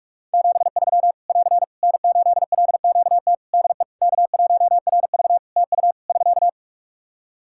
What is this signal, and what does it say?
Morse code, heard as a keyed tone, about 46 wpm, 700 Hz. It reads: 73 <AR> N8LQT DE K1RV TU <SK>
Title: K1USN SST QSO Practice 45wpm